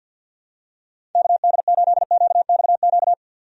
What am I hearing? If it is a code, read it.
Morse code, heard as a keyed tone, about 52 wpm, 700 Hz. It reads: KD8QXQ
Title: USA Call Signs - 6 Characters - 50wpm